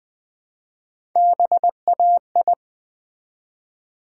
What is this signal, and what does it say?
Morse code, heard as a keyed tone, about 20 wpm, 700 Hz. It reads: BAI
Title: Build A Word - 1 Letter At A Time - 4-Letter Words - 20wpm